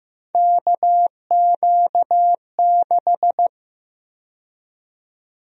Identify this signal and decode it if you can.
Morse code, heard as a keyed tone, about 15 wpm, 700 Hz. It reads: KQ6
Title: USA Call Sign Prefixes - Repeated 2x - 15wpm